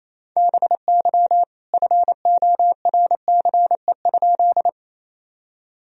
Morse code, heard as a keyed tone, about 28 wpm, 700 Hz. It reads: BY FORCE?
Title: Sentences from Top 1000 Words - Review Part 3 - 28wpm